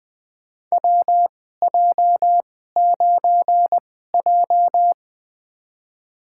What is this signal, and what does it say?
Morse code, heard as a keyed tone, about 20 wpm, spaced as for 15 wpm, 700 Hz. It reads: WJ9J
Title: USA Call Signs - 4 Characters - 2x Character Spacing - 20wpm